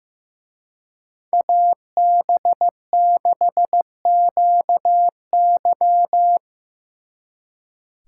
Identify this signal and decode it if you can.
Morse code, heard as a keyed tone, about 18 wpm, 700 Hz. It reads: AB6QY
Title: Speed Racing - USA Call Signs 15wpm